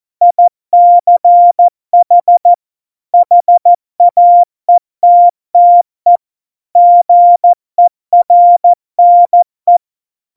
Morse code, heard as a keyed tone, about 14 wpm, 700 Hz. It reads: ICH HAETTE GERNE
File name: Morsekurs / Witze / Witz 03 April 2018, 70 F80